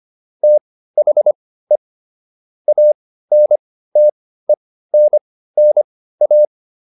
Morse code, heard as a keyed tone, about 25 wpm, spaced as for 15 wpm, 600 Hz. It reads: THE ANTENNA